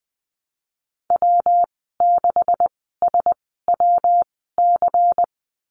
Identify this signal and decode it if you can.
Morse code, heard as a keyed tone, about 20 wpm, spaced as for 15 wpm, 700 Hz. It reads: W6SWC
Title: 161 Lesson 26g - Letter g - Call Signs 20wpm